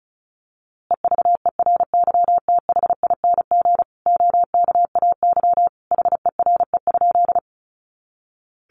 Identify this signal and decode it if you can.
Morse code, heard as a keyed tone, about 34 wpm, 700 Hz. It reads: EVERYTHING OKAY HERE?
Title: A Sound Of Thunder 35wpm